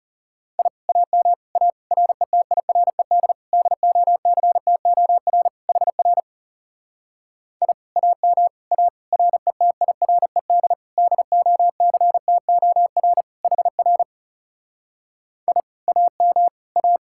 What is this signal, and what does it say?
Morse code, heard as a keyed tone, about 38 wpm, 700 Hz. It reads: I AM A RETIRED DOCTOR HR I AM A RETIRED DOCTOR HR I AM A